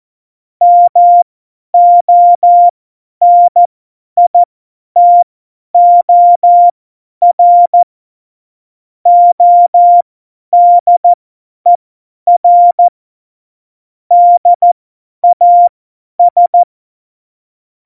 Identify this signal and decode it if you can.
Morse code, heard as a keyed tone, about 14 wpm, spaced as for 10 wpm, 700 Hz. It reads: MONITOR ODER DAS
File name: Morsekurs / Witze / Witz 21. Januar 2020, 50 F70